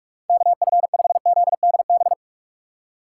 Morse code, heard as a keyed tone, about 44 wpm, 700 Hz. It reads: KF5ZDB